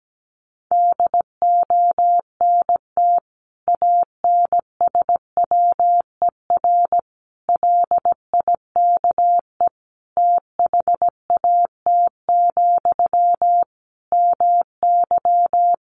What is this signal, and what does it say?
Morse code, heard as a keyed tone, about 17 wpm, 700 Hz. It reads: DONT ANSWER LIKE THAT, MY
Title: Sentences from Top 400 Words 17wpm